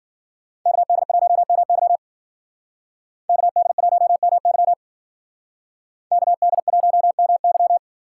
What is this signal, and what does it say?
Morse code, heard as a keyed tone, about 52 wpm, 700 Hz. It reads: KD1MY KD1MY KD1MY